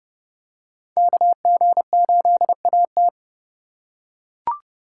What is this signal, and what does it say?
Morse code, heard as a keyed tone, about 29 wpm, 700 Hz. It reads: KG8AT